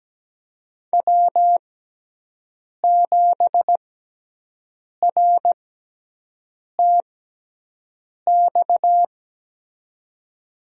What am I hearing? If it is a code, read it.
Morse code, heard as a keyed tone, about 17 wpm, spaced as for 6 wpm, 700 Hz. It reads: W7RTX
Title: USA Call Signs - 6x Character Spacing - 17wpm